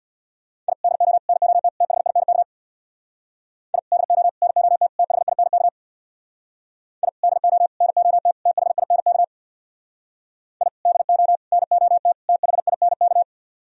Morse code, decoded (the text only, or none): I DO NOT THINK I DO NOT THINK I DO NOT THINK I DO NOT THINK